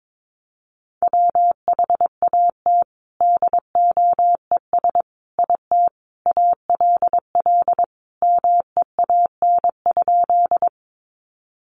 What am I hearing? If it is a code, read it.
Morse code, heard as a keyed tone, about 22 wpm, 700 Hz. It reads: WHAT DOES IT ALL MEAN?
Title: Sentences from Top 200 Words 22wpm